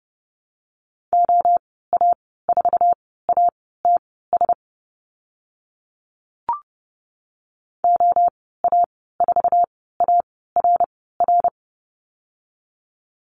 Morse code, heard as a keyed tone, about 30 wpm, spaced as for 17 wpm, 700 Hz. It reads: OA4ATS OA4ARR
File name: Lesson 14g - Number 2 - Call Signs 30wpm